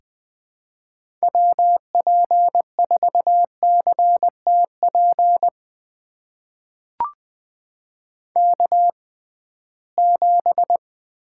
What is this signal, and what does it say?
Morse code, heard as a keyed tone, about 20 wpm, spaced as for 15 wpm, 700 Hz. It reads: WP4CTP K7
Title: USA Call Signs - 6x Character Spacing - 20wpm